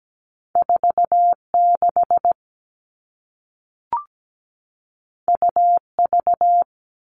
Morse code, heard as a keyed tone, about 17 wpm, 700 Hz. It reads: 46 UV
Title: All Cut Numbers - 2-Digit 17wpm